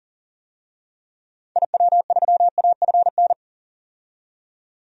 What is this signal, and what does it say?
Morse code, heard as a keyed tone, about 40 wpm, 700 Hz. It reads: IW3AFN
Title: DXCC-100 Call Signs - 6 Characters - 40wpm